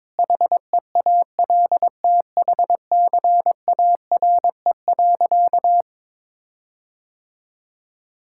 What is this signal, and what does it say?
Morse code, heard as a keyed tone, about 22 wpm, 700 Hz. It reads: HEALTHCARE.
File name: Sentences with 7-Word Phrases - 22wpm